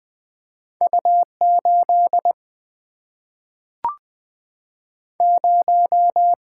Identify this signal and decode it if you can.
Morse code, heard as a keyed tone, about 20 wpm, 700 Hz. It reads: U 8 0
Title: Single-Letter Single-Number 20wpm